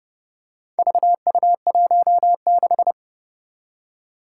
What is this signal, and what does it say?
Morse code, heard as a keyed tone, about 30 wpm, 700 Hz. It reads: VU16
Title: Four Challenging Letter-Number Combinations 30wpm